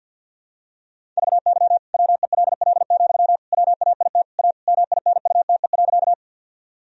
Morse code, heard as a keyed tone, about 50 wpm, 700 Hz. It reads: VY WELL, WAIT A MINUTE.
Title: Sentences from Top 500 Words and Abbreviations 50wpm